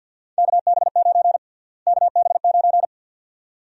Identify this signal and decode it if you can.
Morse code, heard as a keyed tone, about 48 wpm, 700 Hz. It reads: KB9 KB9
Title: USA Call Sign Prefixes - Repeated 2x - 50wpm